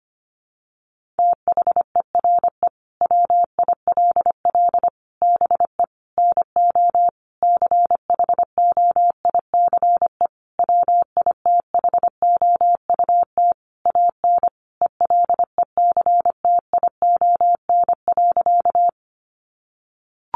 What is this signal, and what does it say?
Morse code, heard as a keyed tone, about 25 wpm, 700 Hz. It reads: THERE WILL BE NO CHOICE WITHOUT AN ELECTION.
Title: Sentences from Top 1500 Words 25wpm